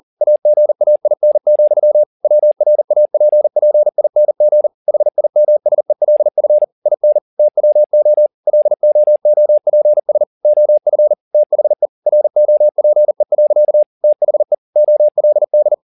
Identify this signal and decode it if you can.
Morse code, heard as a keyed tone, about 40 wpm, 600 Hz. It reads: AGAIN, WRAPPING HIMSELF IN TWO LOOPS OF THE ROPE. THE OLD
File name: the old man and the sea 2 40wpm